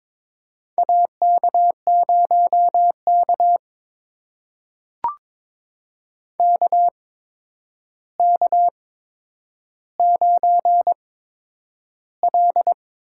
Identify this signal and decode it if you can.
Morse code, heard as a keyed tone, about 22 wpm, spaced as for 9 wpm, 700 Hz. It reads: AK0K KK9L
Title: USA Call Signs - 4 Characters - 8x Character Spacing - 22wpm